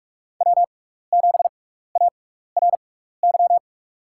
Morse code, heard as a keyed tone, about 44 wpm, spaced as for 15 wpm, 700 Hz. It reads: W7ARY
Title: USA Call Signs - 5 Characters - 6x Character Spacing - 45wpm